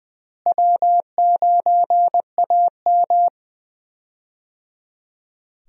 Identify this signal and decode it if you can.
Morse code, heard as a keyed tone, about 20 wpm, 700 Hz. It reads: W9AM
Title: USA Call Signs - 4 Characters - Repeated 3x - 20wpm